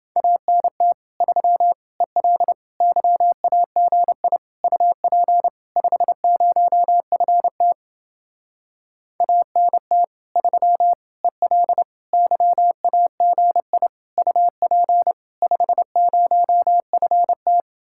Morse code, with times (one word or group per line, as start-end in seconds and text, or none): 0.2-0.9 ANT
1.2-1.7 3
2.0-2.5 EL
2.8-4.4 YAGI
4.6-5.5 UP
5.8-7.7 50FT
9.2-10.0 ANT
10.4-10.9 3
11.2-11.8 EL
12.1-13.9 YAGI
14.2-15.1 UP
15.4-17.6 50FT